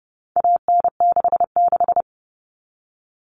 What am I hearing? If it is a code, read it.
Morse code, heard as a keyed tone, about 30 wpm, 700 Hz. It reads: AN66
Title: All Cut Numbers - 4-Digit 30wpm